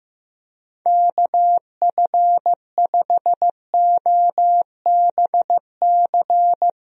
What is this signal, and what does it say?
Morse code, heard as a keyed tone, about 16 wpm, 700 Hz. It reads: KF5OBC
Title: USA Call Signs - 6 Characters - Repeated 2x - 15wpm